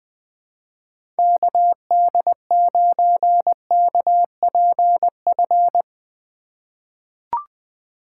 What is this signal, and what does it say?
Morse code, heard as a keyed tone, about 20 wpm, 700 Hz. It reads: KD9KPF
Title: USA Call Signs - 6 Characters - 3x Character Spacing - 20wpm